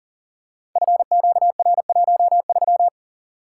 Morse code, heard as a keyed tone, about 42 wpm, 700 Hz. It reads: FQR13
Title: Code Groups - 5 Letter:Numbers - 40wpm